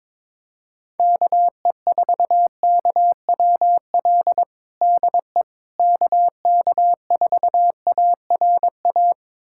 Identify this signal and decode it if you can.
Morse code, heard as a keyed tone, about 22 wpm, 700 Hz. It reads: KE4KWL DE KK4ARA
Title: USA Callsign DE Callsign - 6 Characters - 22wpm